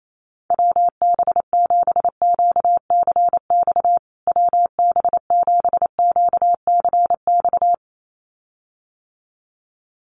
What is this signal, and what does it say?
Morse code, heard as a keyed tone, about 28 wpm, 700 Hz. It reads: WB7QCX WB7QCX
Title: USA Call Signs Repeated 2x 28wpm